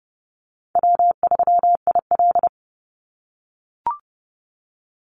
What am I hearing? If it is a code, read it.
Morse code, heard as a keyed tone, about 30 wpm, 700 Hz. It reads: W3IL
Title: Easy Call Signs - 30wpm